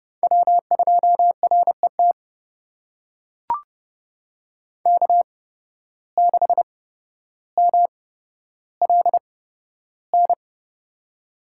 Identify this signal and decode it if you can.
Morse code, heard as a keyed tone, about 30 wpm, spaced as for 13 wpm, 700 Hz. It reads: W2RET K6MLN
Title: USA Call Signs - 8x Character Spacing - 30wpm